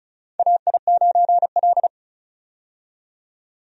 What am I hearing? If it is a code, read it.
Morse code, heard as a keyed tone, about 36 wpm, 700 Hz. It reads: AI9L
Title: USA Call Signs - 4 Characters - 35wpm